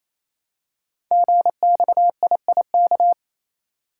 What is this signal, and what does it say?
Morse code, heard as a keyed tone, about 29 wpm, 700 Hz. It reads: GXIIK
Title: Code Groups - 5 Letters - 28wpm